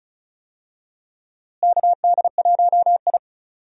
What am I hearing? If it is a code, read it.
Morse code, heard as a keyed tone, about 36 wpm, 700 Hz. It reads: KD1I
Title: USA Call Signs - 4 Characters - Repeated 3x - 35wpm